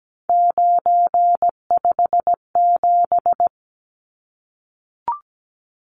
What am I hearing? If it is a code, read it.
Morse code, heard as a keyed tone, about 17 wpm, 700 Hz. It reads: 957
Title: Numbers - 3-Digit 17wpm